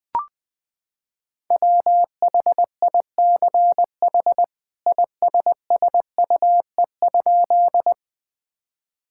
Morse code, heard as a keyed tone, about 20 wpm, 700 Hz. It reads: WHICH ISSUE?
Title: Sentences from Top 1000 Words - Review Part 6 - 20wpm